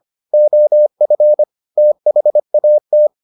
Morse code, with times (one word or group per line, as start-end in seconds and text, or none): 0.3-1.4 OF
1.8-3.1 THAT